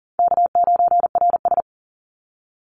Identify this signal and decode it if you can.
Morse code, heard as a keyed tone, about 40 wpm, 700 Hz. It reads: K9RS
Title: Easy Call Signs Repeated 2x - 40wpm